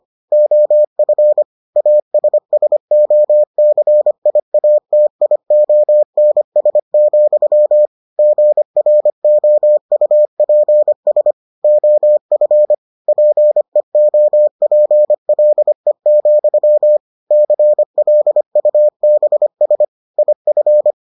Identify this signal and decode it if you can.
Morse code, heard as a keyed tone, about 25 wpm, 600 Hz. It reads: OF ASSOCIATIONS, GROUPS OF PEOPLE, CLUBS IF